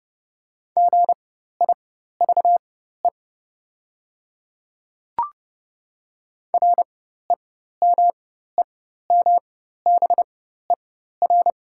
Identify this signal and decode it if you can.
Morse code, heard as a keyed tone, about 30 wpm, spaced as for 14 wpm, 700 Hz. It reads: GIVE REMEMBER
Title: 4x Character Spacing - Top 500 Words - 30wpm